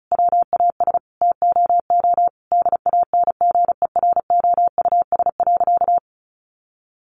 Morse code, heard as a keyed tone, about 34 wpm, 700 Hz. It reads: WAS TOO DANGEROUS.
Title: Sentences from Top 1200 Words 35wpm